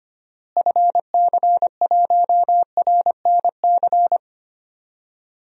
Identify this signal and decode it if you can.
Morse code, heard as a keyed tone, about 25 wpm, 700 Hz. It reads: FC1RNC
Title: DXCC-100 Call Signs - 6 Characters - Repeated 3x - 25wpm